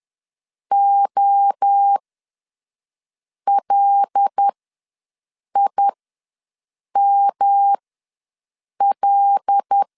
Morse code, heard as a keyed tone, about 10 wpm, spaced as for 5 wpm, 800 Hz. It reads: OLIML
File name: LEZIONI / 5 LBVC / 5 L ws 10wpm 3636